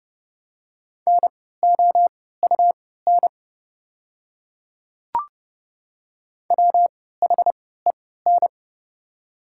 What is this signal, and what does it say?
Morse code, heard as a keyed tone, about 30 wpm, spaced as for 17 wpm, 700 Hz. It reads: NOUN WHEN
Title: 3x Character Spacing - Top 500 Words - 30wpm